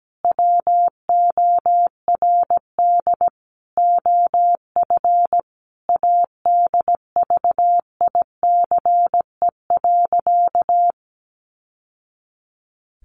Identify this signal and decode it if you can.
Morse code, heard as a keyed tone, about 17 wpm, 700 Hz. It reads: WORD OF ADVICE.
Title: Sentences from Top 1100 Words 17wpm